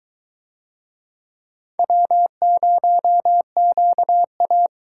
Morse code, heard as a keyed tone, about 22 wpm, 700 Hz. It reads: W0QA